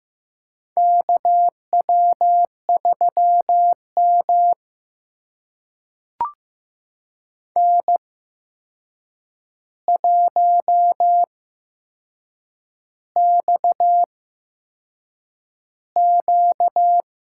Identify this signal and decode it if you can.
Morse code, heard as a keyed tone, about 15 wpm, spaced as for 6 wpm, 700 Hz. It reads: KW3M N1XQ
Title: USA Call Signs - 4 Characters - 8x Character Spacing - 15wpm